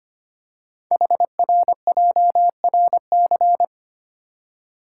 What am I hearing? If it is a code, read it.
Morse code, heard as a keyed tone, about 25 wpm, 700 Hz. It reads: HRJRC
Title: Code Groups - 5 Letters - 25wpm